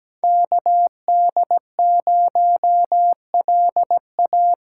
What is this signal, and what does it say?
Morse code, heard as a keyed tone, about 17 wpm, 700 Hz. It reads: KD0LA
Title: USA Call Signs - 6 Characters - 8x Character Spacing - 17wpm